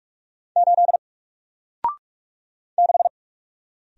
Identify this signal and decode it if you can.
Morse code, heard as a keyed tone, about 44 wpm, 700 Hz. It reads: 8 6
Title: Single Character (Letter:Number:Punctuation:Prosign) 45wpm